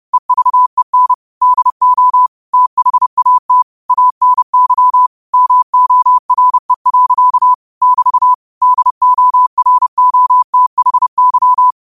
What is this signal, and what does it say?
Morse code, heard as a keyed tone, about 30 wpm, 1000 Hz. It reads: EVEN DO THAT ANY MORE. <BT> DOROTHY
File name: qotdfiles / 30wpm